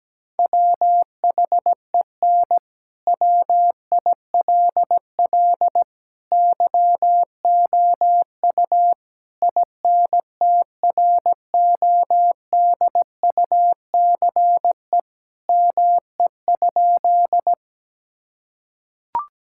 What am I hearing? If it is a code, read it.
Morse code, heard as a keyed tone, about 17 wpm, 700 Hz. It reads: WHEN WILL YOU INTRODUCE ME?
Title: Sentences from Top 1600 Words 17wpm